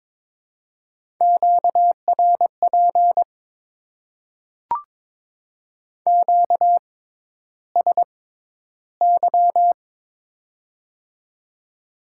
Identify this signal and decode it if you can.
Morse code, heard as a keyed tone, about 21 wpm, spaced as for 11 wpm, 700 Hz. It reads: QRP QSY
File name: Top 90 QSO Elements - 6x Character Spacing - 22wpm